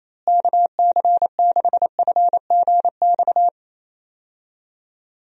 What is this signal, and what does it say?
Morse code, heard as a keyed tone, about 28 wpm, 700 Hz. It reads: KC6FGX
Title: USA Call Signs - 6 Characters - Repeated 3x - 28wpm